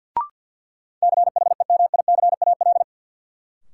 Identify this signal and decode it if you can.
Morse code, heard as a keyed tone, about 53 wpm, 700 Hz. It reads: CHEMICAL
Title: Sentences from Top 1200 Words - New Word Drill - 50wpm